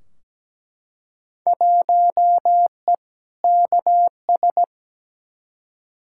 Morse code, heard as a keyed tone, about 17 wpm, 700 Hz. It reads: 1E KS
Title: Field Day Exchanges - 17wpm